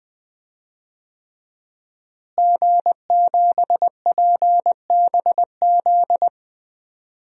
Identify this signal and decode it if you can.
Morse code, heard as a keyed tone, about 20 wpm, 700 Hz. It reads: G7PBZ
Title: DXCC-100 Call Signs - 5 Characters - Repeated 3x - 20wpm